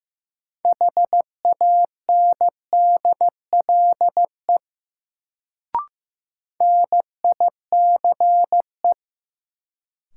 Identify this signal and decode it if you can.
Morse code, heard as a keyed tone, about 15 wpm, 700 Hz. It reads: HANDLE NICE
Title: Sentences from Top 1100 Words - New Word Drill - 15wpm